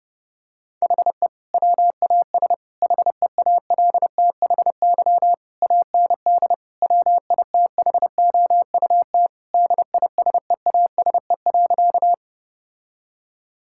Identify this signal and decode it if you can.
Morse code, heard as a keyed tone, about 30 wpm, 700 Hz. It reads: HE WAS HEALTHY AND WITHOUT DISEASE.
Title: Sentences from Top 1000 Words - Review Part 6 - 30wpm